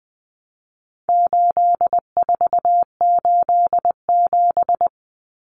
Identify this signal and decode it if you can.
Morse code, heard as a keyed tone, about 20 wpm, 700 Hz. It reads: 8487
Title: Common Cut Numbers - 4-Digit 20wpm